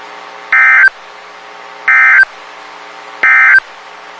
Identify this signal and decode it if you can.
Emergency Alert System (SAME) tones, heard as harsh buzzing bursts. It is heard from 0.4 to 3.7 s.
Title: WXK91 NOAA Weather Radio